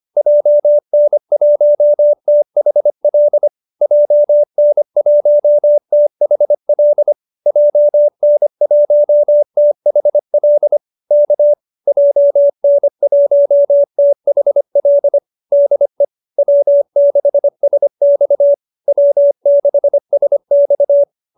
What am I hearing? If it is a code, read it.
Morse code, heard as a keyed tone, about 25 wpm, 590 Hz. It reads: JN1THL JN1THL JN1THL K JN1THL DE W6SX W6SX
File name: QSO305 25